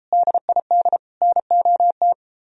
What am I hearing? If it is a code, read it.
Morse code, heard as a keyed tone, about 33 wpm, 700 Hz. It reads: DID NOT